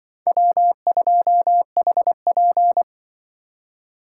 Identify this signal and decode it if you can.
Morse code, heard as a keyed tone, about 24 wpm, 700 Hz. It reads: W2HP
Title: Speed Racing - USA Call Signs 20wpm